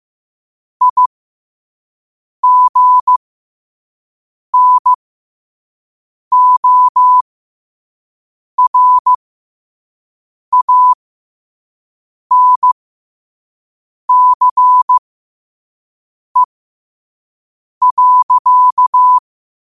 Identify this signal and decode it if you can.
Morse code, heard as a keyed tone, about 15 wpm, spaced as for 5 wpm, 1000 Hz. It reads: IGNORANCE.
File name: qotdfiles / 5wpm